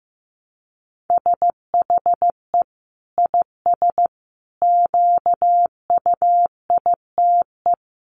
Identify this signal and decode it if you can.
Morse code, heard as a keyed tone, about 15 wpm, 700 Hz. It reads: SHE IS QUITE
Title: Sentences from Top 1300 Words 15wpm